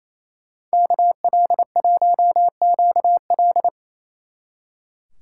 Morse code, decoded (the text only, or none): KL1QL